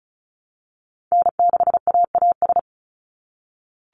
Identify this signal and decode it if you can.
Morse code, heard as a keyed tone, about 36 wpm, 700 Hz. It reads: N6AAS
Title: USA Call Signs 35wpm